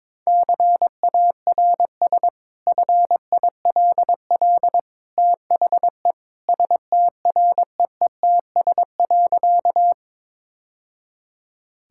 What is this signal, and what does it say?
Morse code, heard as a keyed tone, about 22 wpm, 700 Hz. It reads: CARS FILL THE STREETS.
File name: practice / 22wpm / Easy Sentences with Hints - 22wpm